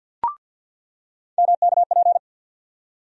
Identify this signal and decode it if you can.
Morse code, heard as a keyed tone, about 50 wpm, 700 Hz. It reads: MKP
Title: USA Call Sign Postfixes - 3 Letters 50wpm